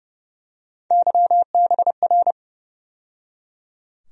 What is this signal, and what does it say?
Morse code, heard as a keyed tone, about 30 wpm, 700 Hz. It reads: YBR
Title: USA Call Sign Postfixes - 3 Letters 30wpm